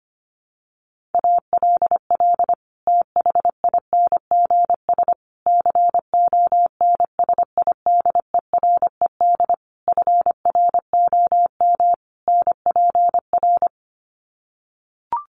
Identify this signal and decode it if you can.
Morse code, heard as a keyed tone, about 25 wpm, 700 Hz. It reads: ALL THINGS CONSIDERED FROM NPR
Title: Sets of 5 Words Encore 25wpm